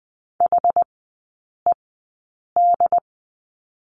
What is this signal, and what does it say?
Morse code, heard as a keyed tone, about 20 wpm, spaced as for 8 wpm, 700 Hz. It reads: HED
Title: 067 Lesson 12e - Letter d - Three Characters 20wpm